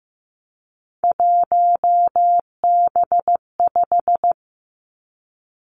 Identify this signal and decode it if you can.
Morse code, heard as a keyed tone, about 15 wpm, 700 Hz. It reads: 1B5
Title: Three Challenging Letter-Number Combinations 15wpm